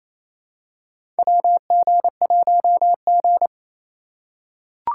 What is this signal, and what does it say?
Morse code, heard as a keyed tone, about 28 wpm, 700 Hz. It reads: WG1G
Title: USA Call Signs - 2x Character Spacing - 28wpm